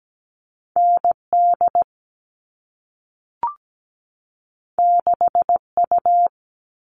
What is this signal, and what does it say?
Morse code, heard as a keyed tone, about 17 wpm, 700 Hz. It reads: ND 6U
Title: All Cut Numbers - 2-Digit 17wpm